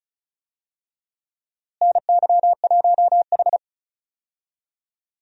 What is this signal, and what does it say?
Morse code, heard as a keyed tone, about 36 wpm, 700 Hz. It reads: NY1H
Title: USA Call Signs - 4 Characters - Repeated 2x - 35wpm